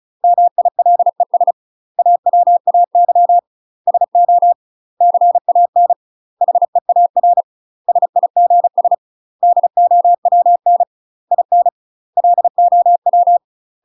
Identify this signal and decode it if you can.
Morse code, heard as a keyed tone, about 36 wpm, 700 Hz. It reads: MILES AWAY SO CAN HEAR SIGS DOWN IN LOW
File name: UK MASTER QSOs TRAINING / 35WPM / UK MASTER QSOs 0018